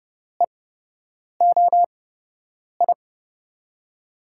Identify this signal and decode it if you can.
Morse code, heard as a keyed tone, about 32 wpm, spaced as for 8 wpm, 700 Hz. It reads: EOI
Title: USA Call Signs - 6 Characters - 8x Character Spacing - 30wpm